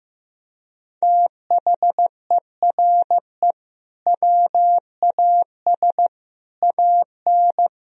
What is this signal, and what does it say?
Morse code, heard as a keyed tone, about 16 wpm, 700 Hz. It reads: THERE WAS AN
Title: Sentences from Top 1000 Words - Review Part 3 - 15wpm